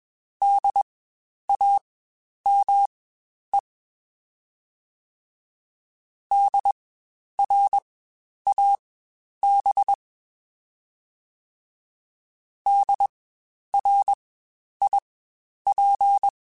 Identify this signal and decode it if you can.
Morse code, heard as a keyed tone, about 21 wpm, spaced as for 10 wpm, 800 Hz. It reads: DAME DRAB DRIP
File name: Spaced Alphas / Spaced Alphas mp3s / Ds sp @20